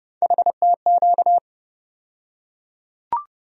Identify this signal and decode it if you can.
Morse code, heard as a keyed tone, about 30 wpm, 700 Hz. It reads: HTQ
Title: USA Call Sign Postfixes - 3 Letters 30wpm